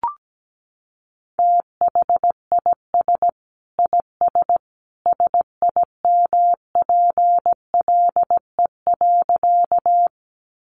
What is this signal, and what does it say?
Morse code, heard as a keyed tone, about 17 wpm, 700 Hz. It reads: THIS IS SIMPLE.
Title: Sentences from Top 400 Words 17wpm